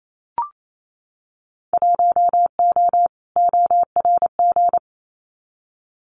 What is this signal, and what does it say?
Morse code, heard as a keyed tone, about 28 wpm, 700 Hz. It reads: 1O ORG
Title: Winter Field Day Exchanges 28wpm